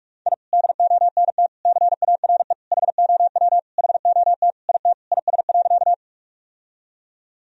Morse code, heard as a keyed tone, about 41 wpm, 700 Hz. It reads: I DONT CARE HOW HOT IT IS.
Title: Sentences from Top 1000 Words - Review Part 3 - 45wpm